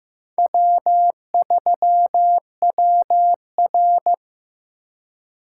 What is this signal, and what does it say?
Morse code, heard as a keyed tone, about 18 wpm, 700 Hz. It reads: W3WR
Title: Speed Racing - USA Call Signs 15wpm